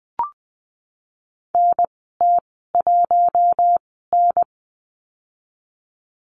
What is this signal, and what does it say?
Morse code, heard as a keyed tone, about 20 wpm, spaced as for 14 wpm, 700 Hz. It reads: NT1N
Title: 055 Lesson 10g - Letter r - Call Signs 20wpm